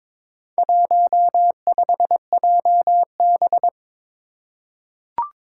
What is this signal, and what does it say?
Morse code, heard as a keyed tone, about 22 wpm, 700 Hz. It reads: 15JB
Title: Four Challenging Letter-Number Combinations 22wpm